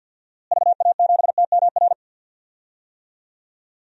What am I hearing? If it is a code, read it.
Morse code, heard as a keyed tone, about 53 wpm, 700 Hz. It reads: VA7TMR
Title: DXCC-100 Call Signs - 6 Characters - Repeated 3x - 50wpm